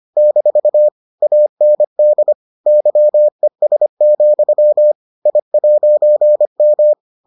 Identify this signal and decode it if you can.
Morse code, heard as a keyed tone, about 25 wpm, 600 Hz. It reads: - AND YES, I'M